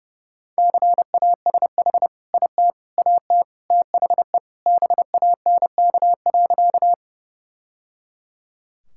Morse code, heard as a keyed tone, about 28 wpm, 700 Hz. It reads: CASH IT AT THE BANK.
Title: Sentences from Top 1000 Words - Review Part 5 - 30wpm